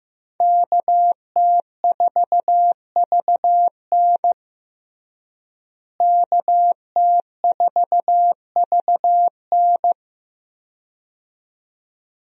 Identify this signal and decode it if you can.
Morse code, heard as a keyed tone, about 16 wpm, 700 Hz. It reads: KT4VN KT4VN
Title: USA Call Signs - 5 Characters - Repeated 2x - 15wpm